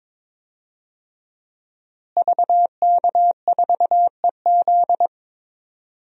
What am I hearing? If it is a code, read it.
Morse code, heard as a keyed tone, about 22 wpm, 700 Hz. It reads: VK4EZ
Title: DXCC-100 Call Signs - 5 Characters - Repeated 3x - 22wpm